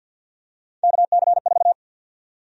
Morse code, heard as a keyed tone, about 51 wpm, 700 Hz. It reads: KC4
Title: USA Call Sign Prefixes - Repeated 2x - 50wpm